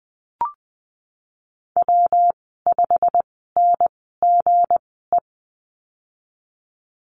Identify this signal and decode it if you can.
Morse code, heard as a keyed tone, about 20 wpm, spaced as for 15 wpm, 700 Hz. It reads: W5NGE
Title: 179 Lesson 29g - Slash - Call Signs 20wpm